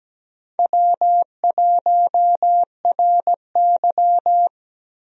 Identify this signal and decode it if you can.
Morse code, heard as a keyed tone, about 17 wpm, 700 Hz. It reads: W1RY
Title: USA Call Signs - 4 Characters - Repeated 3x - 17wpm